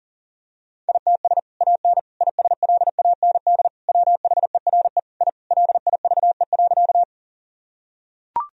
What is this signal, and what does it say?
Morse code, heard as a keyed tone, about 40 wpm, 700 Hz. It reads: ITS AN ISLAND WHERE I LIVE.
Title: Sentences from Top 1000 Words - Review Part 3 - 40wpm